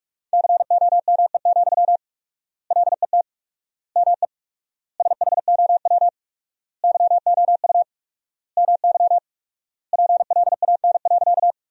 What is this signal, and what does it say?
Morse code, heard as a keyed tone, about 47 wpm, 700 Hz. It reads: COME, LET ME SHOW YOU MY PLAN.
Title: 4x Word-Spacing - Sentences from Top 500 Words - 45wpm